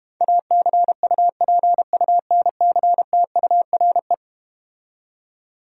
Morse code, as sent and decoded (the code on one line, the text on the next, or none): .- -.-. ..- .--. ..- -. -.-. - ..- .-. .
ACUPUNCTURE